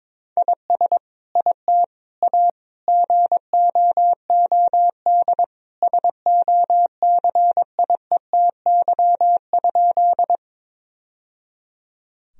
Morse code, heard as a keyed tone, about 21 wpm, 700 Hz. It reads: IS IT A GOOD SOCIETY?
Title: Sentences from Top 1000 Words - Review Part 4 - 22wpm